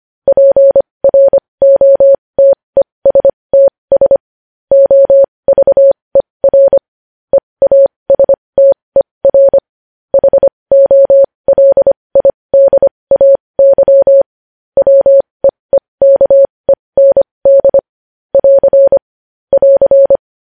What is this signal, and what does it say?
Morse code, heard as a keyed tone, about 25 wpm, spaced as for 20 wpm, 550 Hz. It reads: PROTESTS OVER EASTER HOLIDAY WEEKEND <AR> <AR>
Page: News Headlines in Morse Code at 20 WPM